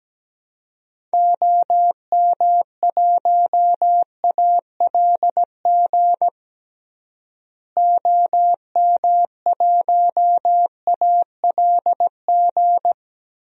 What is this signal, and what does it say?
Morse code, heard as a keyed tone, about 17 wpm, 700 Hz. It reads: OM1ALG OM1ALG
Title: DXCC-100 Call Signs - 6 Characters - Repeated 2x - 17wpm